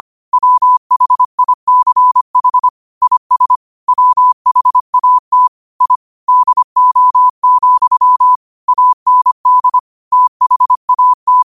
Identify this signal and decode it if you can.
Morse code, heard as a keyed tone, about 25 wpm, 1000 Hz. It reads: WHICH IS WHAT I DO, AND THAT
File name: qotdfiles / 25wpm